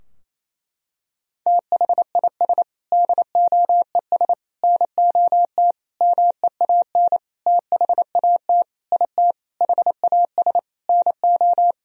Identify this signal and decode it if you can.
Morse code, heard as a keyed tone, about 28 wpm, 700 Hz. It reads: THIS DOES NOT MEAN THAT IT HAS NO
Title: Sentences from Top 1000 Words - Review Part 4 - 28wpm